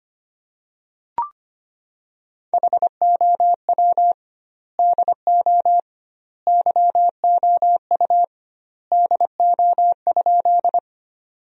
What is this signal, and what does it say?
Morse code, heard as a keyed tone, about 25 wpm, 700 Hz. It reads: HOW DO YOU DO?
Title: 2x Word-Spacing - Sentences from Top 100 Words - 25wpm